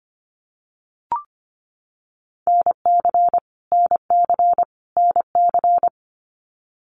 Morse code, heard as a keyed tone, about 25 wpm, 700 Hz. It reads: NC NC NC
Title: US States Repeated 3x 25wpm